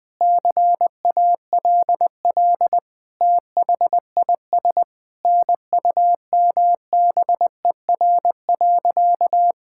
Morse code, heard as a keyed tone, about 20 wpm, 700 Hz. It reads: CALL THIS NUMBER.
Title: Sentences with 5-Word Phrases - Spoken Beginning - 20wpm